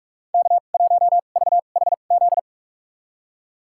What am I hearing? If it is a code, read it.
Morse code, heard as a keyed tone, about 46 wpm, spaced as for 32 wpm, 700 Hz. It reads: K1VHZ
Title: USA Call Signs - 2x Character Spacing - 45wpm